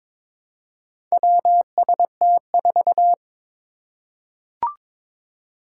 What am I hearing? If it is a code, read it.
Morse code, heard as a keyed tone, about 23 wpm, 700 Hz. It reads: WST4
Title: Four Challenging Letter-Number Combinations 22wpm